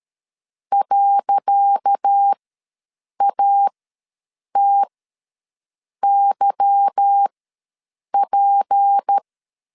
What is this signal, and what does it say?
Morse code, heard as a keyed tone, about 13 wpm, spaced as for 7 wpm, 800 Hz. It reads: .ATYP
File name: LEZIONI / 10 SIGNS / 10C DOT ws 12wpm 3548